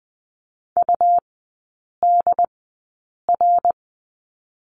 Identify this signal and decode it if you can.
Morse code, heard as a keyed tone, about 20 wpm, spaced as for 8 wpm, 700 Hz. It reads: UDR
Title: 121 Lesson 20e - Number 3 - Three Characters 20wpm